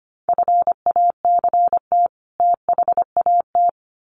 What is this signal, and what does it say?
Morse code, heard as a keyed tone, about 25 wpm, 700 Hz. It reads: FACT THAT
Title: Sets of 2 Words 25wpm